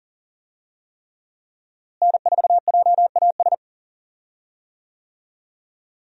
Morse code, heard as a keyed tone, about 37 wpm, 700 Hz. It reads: N4JAS
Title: USA Call Signs - 5 Characters - Repeated 3x - 40wpm